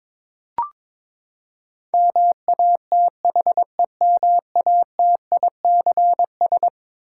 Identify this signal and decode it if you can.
Morse code, heard as a keyed tone, about 21 wpm, 700 Hz. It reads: MATHEMATICS
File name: practice / 22wpm / Eleven Letter Words - 22wpm